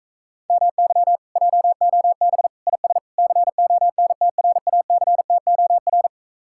Morse code, heard as a keyed tone, about 44 wpm, 700 Hz. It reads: MY JOB IS CONTRACTOR